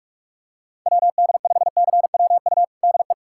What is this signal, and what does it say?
Morse code, heard as a keyed tone, about 46 wpm, 690 Hz. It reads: WD5CWU DE
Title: USA Callsign DE Callsign 45wpm